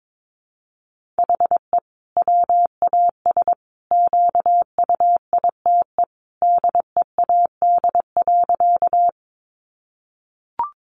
Morse code, heard as a keyed tone, about 22 wpm, 700 Hz. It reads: HE WAS QUITE DEAD.
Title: Sentences from Top 1500 Words 22wpm